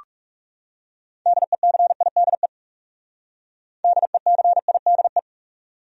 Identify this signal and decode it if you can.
Morse code, heard as a keyed tone, about 37 wpm, 700 Hz. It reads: DECIDE DECIDE
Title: Speed-Racing Top 500 Words 30wpm